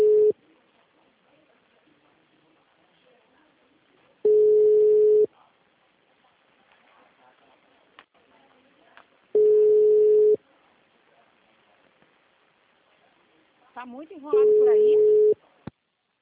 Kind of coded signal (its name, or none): telephone call progress tone